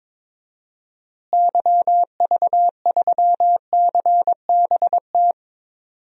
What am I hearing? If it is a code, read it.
Morse code, heard as a keyed tone, about 22 wpm, 700 Hz. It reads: YV3CBT
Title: DXCC-100 Call Signs - 6 Characters - Repeated 3x - 22wpm